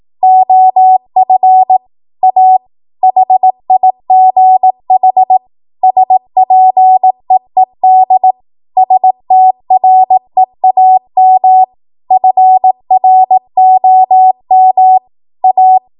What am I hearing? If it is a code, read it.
Morse code, heard as a keyed tone, about 18 wpm, 750 Hz. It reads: OF A HIGH SPEED STREAM FROM A
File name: Morse / 18WPM